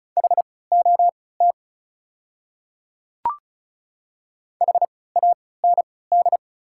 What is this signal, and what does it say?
Morse code, heard as a keyed tone, about 34 wpm, spaced as for 20 wpm, 700 Hz. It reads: HOT HAND
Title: 3x Character Spacing - Top 300 Words - 35wpm